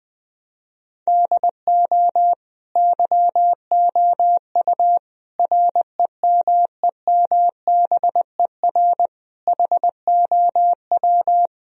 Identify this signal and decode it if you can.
Morse code, heard as a keyed tone, about 20 wpm, 700 Hz. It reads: DO YOU REMEMBER HOW
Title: Sentences from Top 1000 Words - Review Part 3 - 20wpm